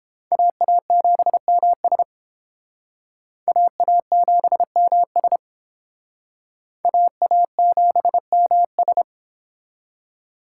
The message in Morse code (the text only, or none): AA7MS AA7MS AA7MS